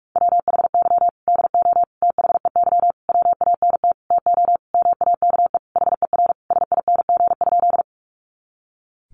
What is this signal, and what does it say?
Morse code, heard as a keyed tone, about 44 wpm, 700 Hz. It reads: WHY DO THEY WANT TO MAKE HER SING?
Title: Sentences from Top 400 Words 45wpm